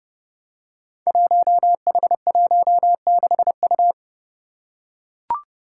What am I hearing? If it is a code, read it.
Morse code, heard as a keyed tone, about 30 wpm, 700 Hz. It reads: 1H16U
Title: Five Challenging Letter-Number Combinations 30wpm